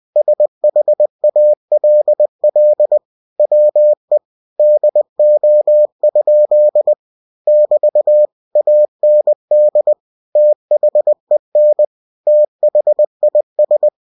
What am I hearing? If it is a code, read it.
Morse code, heard as a keyed tone, about 20 wpm, 600 Hz. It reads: SHALL WE DO? <BT> AND THEN THIS
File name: cw-books / winnie-the-pooh / winnie-20wpm-page-0099